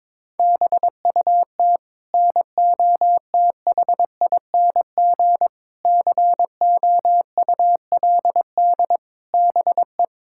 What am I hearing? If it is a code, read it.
Morse code, heard as a keyed tone, about 22 wpm, 700 Hz. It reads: BUT NOTHING COULD BE
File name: practice / 22wpm / Sets of 8 Words - 22wpm